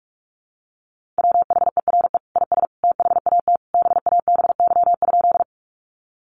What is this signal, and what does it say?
Morse code, heard as a keyed tone, about 44 wpm, 700 Hz. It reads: WHERE IS THAT BABY?
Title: Sentences from Top 1300 Words 45wpm